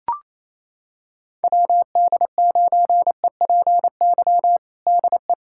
Morse code, heard as a keyed tone, about 28 wpm, 700 Hz. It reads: WD9EPY DE
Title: USA Callsign DE Callsign 28wpm